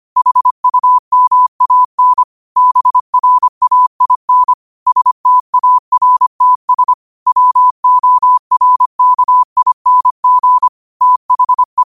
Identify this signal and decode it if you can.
Morse code, heard as a keyed tone, about 25 wpm, 1000 Hz. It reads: HUMAN BRAIN STARTS WORKING THE
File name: qotdfiles / 25wpm